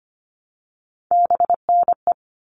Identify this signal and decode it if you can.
Morse code, heard as a keyed tone, about 25 wpm, 700 Hz. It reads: BNE
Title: All Cut Numbers - 3-Digit 25wpm